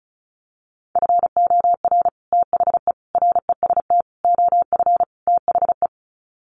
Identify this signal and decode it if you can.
Morse code, heard as a keyed tone, about 34 wpm, 700 Hz. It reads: FOR THE REST OF THE
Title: Sets of 5 Words 35wpm